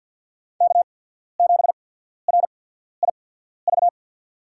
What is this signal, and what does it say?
Morse code, heard as a keyed tone, about 47 wpm, spaced as for 13 wpm, 700 Hz. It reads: K7RIV
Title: USA Call Signs - 5 Characters - 8x Character Spacing - 50wpm